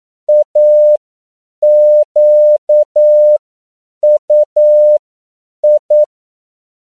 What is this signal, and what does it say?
Morse code, heard as a keyed tone, about 9 wpm, spaced as for 7 wpm, 600 Hz. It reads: AQUI